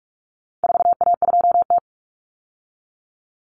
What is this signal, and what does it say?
Morse code, heard as a keyed tone, about 44 wpm, 700 Hz. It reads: 4A2T
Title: Common Cut Numbers - 4-Digit 45wpm